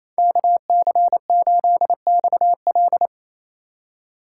Morse code, heard as a keyed tone, about 28 wpm, 700 Hz. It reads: KC8XL
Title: Code Groups - 5 Letter:Numbers - 28wpm